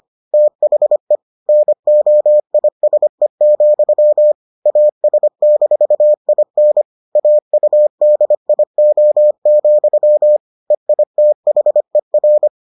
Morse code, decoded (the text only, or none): THE NOISE, AS-IN AUDIO, EITHER